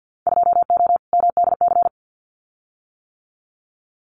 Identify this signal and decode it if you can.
Morse code, heard as a keyed tone, about 50 wpm, 700 Hz. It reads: 2O MDC
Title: Winter Field Day Exchanges 50wpm